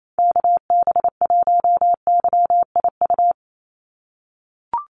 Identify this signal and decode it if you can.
Morse code, heard as a keyed tone, about 28 wpm, 700 Hz. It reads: KB1YIU
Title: USA Call Signs 28wpm